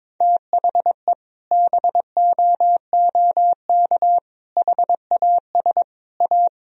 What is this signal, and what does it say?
Morse code, heard as a keyed tone, about 22 wpm, 700 Hz. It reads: THE BOOK HAS A
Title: Sentences from Top 1000 Words - Review Part 3 - 22wpm